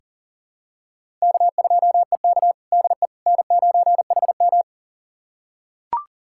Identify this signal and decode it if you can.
Morse code, heard as a keyed tone, about 40 wpm, 700 Hz. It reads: K2EK DE N9HM
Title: USA Callsign DE Callsign - 4 Characters - 40wpm